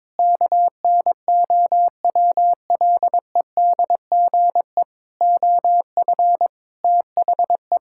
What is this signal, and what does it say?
Morse code, heard as a keyed tone, about 22 wpm, 700 Hz. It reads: KNOWLEDGE OF THE
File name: Sets of 9 Words - 22wpm